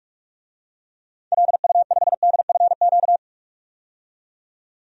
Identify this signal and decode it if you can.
Morse code, heard as a keyed tone, about 45 wpm, 700 Hz. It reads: LU5DFQ
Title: DXCC-100 Call Signs - 6 Characters - 45wpm